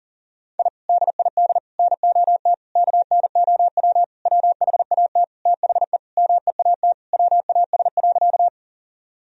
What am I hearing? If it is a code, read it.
Morse code, heard as a keyed tone, about 40 wpm, 700 Hz. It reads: I DID NOT KNOW WHAT THE MEAT WAS.
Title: Sentences from Top 1000 Words - Review Part 3 - 40wpm